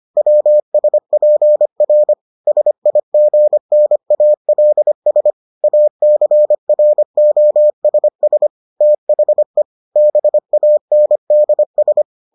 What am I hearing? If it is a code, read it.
Morse code, heard as a keyed tone, about 25 wpm, 600 Hz. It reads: WSPR SIGNALS ACROSS THE BANDS